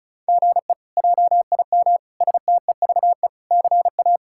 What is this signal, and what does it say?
Morse code, heard as a keyed tone, about 36 wpm, 700 Hz. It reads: GE JIM STEVE CA
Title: K1USN SST QSO Practice - 35wpm